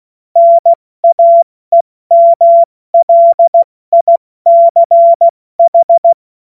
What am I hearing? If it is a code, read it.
Morse code, heard as a keyed tone, about 16 wpm, 680 Hz. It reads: NAEMLICH
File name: Morsekurs / Witze / Witz 29. Juni 2021, 70 F80